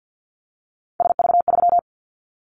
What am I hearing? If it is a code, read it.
Morse code, heard as a keyed tone, about 50 wpm, 700 Hz. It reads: SV3
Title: Callsign Prefixes for Top 50 DXCC - 50wpm